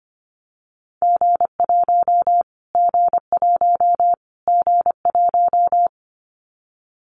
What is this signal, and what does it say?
Morse code, heard as a keyed tone, about 25 wpm, 700 Hz. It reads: G1 G1 G1
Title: Callsign Prefixes for Top 25 DXCC - Repeated 3x - 25wpm